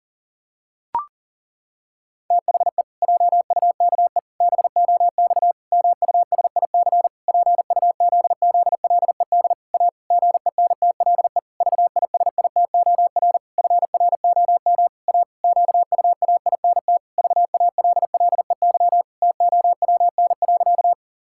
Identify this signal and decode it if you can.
Morse code, heard as a keyed tone, about 40 wpm, 700 Hz. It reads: THE JUKE BOX MUSIC PUZZLED A GENTLE VISITOR FROM A QUAINT VALLEY TOWN.
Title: Pangram Challenge v2 - 40wpm